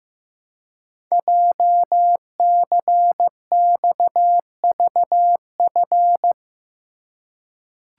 Morse code, heard as a keyed tone, about 15 wpm, 700 Hz. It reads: JCXVF
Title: Code Groups - 5 Letters - 15wpm